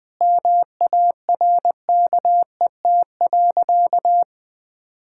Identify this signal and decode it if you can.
Morse code, heard as a keyed tone, about 21 wpm, 700 Hz. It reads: MARKET.
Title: Sentences from Top 1000 Words - Review Part 6 - 20wpm